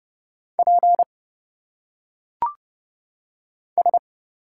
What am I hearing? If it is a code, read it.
Morse code, heard as a keyed tone, about 30 wpm, 700 Hz. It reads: P S
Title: Single Letters - Phonetic Pronunciation 30wpm